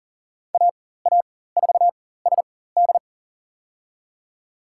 Morse code, heard as a keyed tone, about 40 wpm, spaced as for 19 wpm, 700 Hz. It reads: AA4SD
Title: USA Call Signs - 5 Characters - 4x Character Spacing - 40wpm